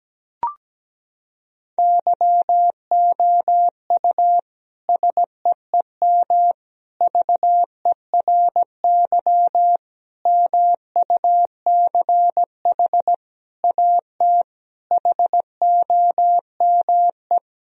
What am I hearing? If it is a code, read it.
Morse code, heard as a keyed tone, about 17 wpm, 700 Hz. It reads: YOU SEEM VERY MUCH AT HOME
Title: Sentences from Top 1000 Words - Review Part 1 - 17wpm